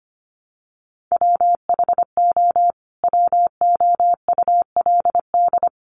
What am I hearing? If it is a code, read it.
Morse code, heard as a keyed tone, about 25 wpm, 700 Hz. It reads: WHO WOULD
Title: Sentences from Top 1200 Words 25wpm